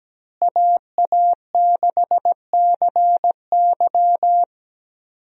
Morse code, heard as a keyed tone, about 17 wpm, 700 Hz. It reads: AA6CY
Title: USA Call Signs - 5 Characters - Repeated 3x - 17wpm